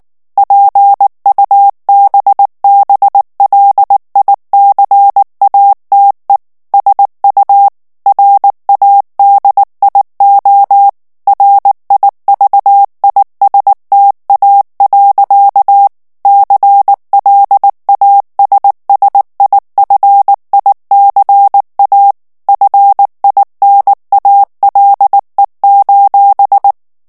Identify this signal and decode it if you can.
Morse code, heard as a keyed tone, about 19 wpm, 800 Hz. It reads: PUBBLICATE SU RADIO RIVISTA. CLASSIFICA FINALE:
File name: morse mp3 / ITALIEN texte clair 2 vitesse 18 mots minute